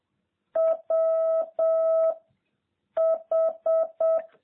Morse code, heard as a keyed tone, about 7 wpm, 660 Hz. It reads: W H